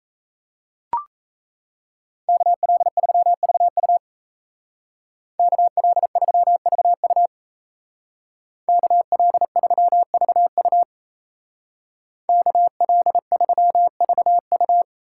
Morse code, heard as a keyed tone, about 34 wpm, 700 Hz. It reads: KL3VU KL3VU KL3VU KL3VU
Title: Speed Racing - USA 2x2 Call Signs 28wpm